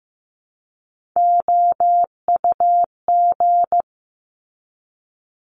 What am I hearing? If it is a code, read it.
Morse code, heard as a keyed tone, about 15 wpm, 700 Hz. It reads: OUG
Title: Top Three Letter Combinations in Words 15wpm